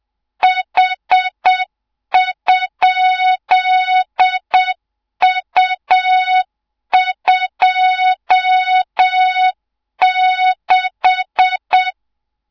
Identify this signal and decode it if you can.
Morse code, heard as a keyed tone, about 7 wpm, 3700 Hz, with noise. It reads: H?U26
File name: Morze-HA2MN / hang / 24 GYAKORLAT